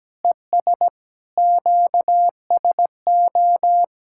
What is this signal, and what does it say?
Morse code, heard as a keyed tone, about 17 wpm, 700 Hz. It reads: ES QSO
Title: K1USN SST QSO Practice 17wpm